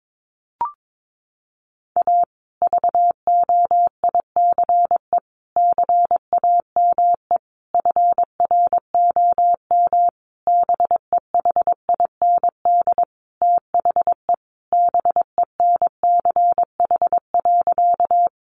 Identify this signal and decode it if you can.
Morse code, heard as a keyed tone, about 22 wpm, 700 Hz. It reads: A VOICE CAME FROM BEHIND THE BENCH.
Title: Sentences from Top 1400 Words 22wpm